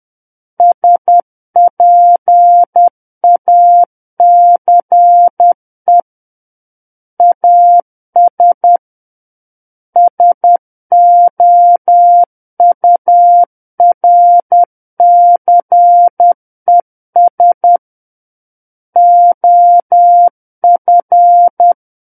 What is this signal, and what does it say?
Morse code, heard as a keyed tone, about 10 wpm, 700 Hz. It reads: SPACE AS SOURCES OF
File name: Book One / 01-Chapter 1 - 10 WPM